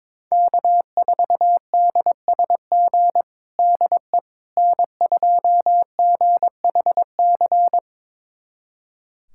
Morse code, heard as a keyed tone, about 22 wpm, 700 Hz. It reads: K4DSG DE N2GHC
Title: USA Callsign DE Callsign - 5 Characters - 22wpm